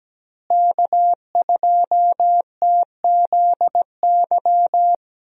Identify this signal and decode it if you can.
Morse code, heard as a keyed tone, about 17 wpm, 700 Hz. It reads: K2TZY
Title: USA Call Signs - 5 Characters - Repeated 3x - 17wpm